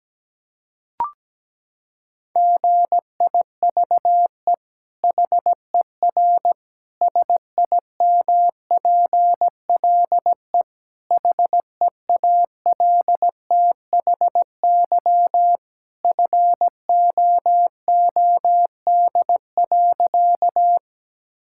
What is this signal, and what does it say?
Morse code, heard as a keyed tone, about 17 wpm, 700 Hz. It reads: GIVE HER SIMPLE HEALTHY FOOD.
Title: Sentences from Top 1000 Words - Review Part 5 - 17wpm